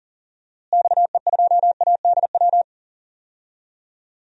Morse code, heard as a keyed tone, about 40 wpm, 700 Hz. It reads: XE2ADW
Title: DXCC-100 Call Signs - 6 Characters - Repeated 3x - 40wpm